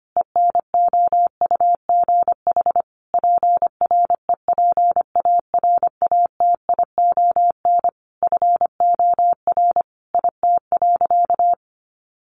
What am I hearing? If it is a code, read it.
Morse code, heard as a keyed tone, about 25 wpm, 700 Hz. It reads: ENOUGH PREPARATION FOR IT.
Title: Sentences from Top 1400 Words 25wpm